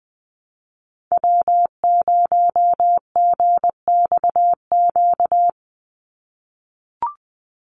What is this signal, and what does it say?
Morse code, heard as a keyed tone, about 20 wpm, 700 Hz. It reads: W0GXQ
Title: Call Signs - 20wpm